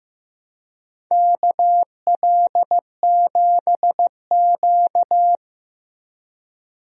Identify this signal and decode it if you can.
Morse code, heard as a keyed tone, about 16 wpm, 700 Hz. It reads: KL7Q
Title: DXCC-100 Call Signs - 4 Characters - Repeated 3x - 15wpm